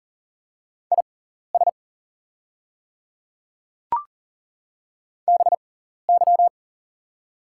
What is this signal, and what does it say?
Morse code, heard as a keyed tone, about 39 wpm, spaced as for 14 wpm, 700 Hz. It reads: IS BY
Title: 6x Character Spacing - Top 100 Words - 40wpm